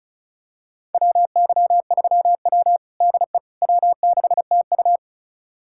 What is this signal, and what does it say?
Morse code, heard as a keyed tone, about 36 wpm, 700 Hz. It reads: WY3W DE W6TU
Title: USA Callsign DE Callsign - 4 Characters - 35wpm